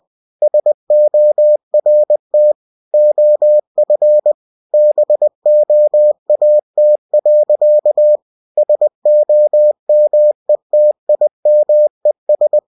Morse code, heard as a keyed tone, about 20 wpm, 600 Hz. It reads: SORT OF BOAT. SOMETIMES
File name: cw-books / winnie-the-pooh / winnie-20wpm-page-0099